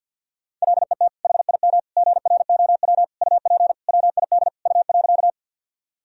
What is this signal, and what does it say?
Morse code, heard as a keyed tone, about 52 wpm, 700 Hz. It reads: LET HIM GROW UP WID U.
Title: Sentences from Top 300 Words and Abbreviations 50wpm